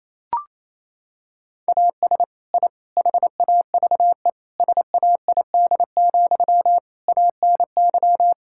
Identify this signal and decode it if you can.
Morse code, heard as a keyed tone, about 28 wpm, 700 Hz. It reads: AS I HAVE SAID, ANY
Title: Sentences from Top 1000 Words - Review Part 6 - 28wpm